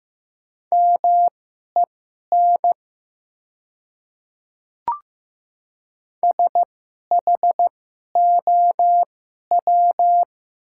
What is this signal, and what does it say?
Morse code, heard as a keyed tone, about 15 wpm, spaced as for 11 wpm, 700 Hz. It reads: MEN SHOW
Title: 2x Character Spacing - Top 300 Words - 15wpm